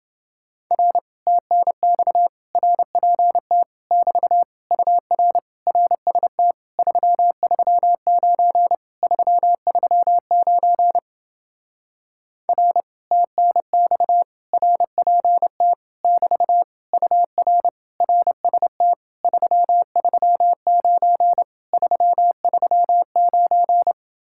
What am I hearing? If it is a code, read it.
Morse code, heard as a keyed tone, about 29 wpm, 700 Hz. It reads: R TNX RPT <BT> UR RST 339 339 R TNX RPT <BT> UR RST 339 339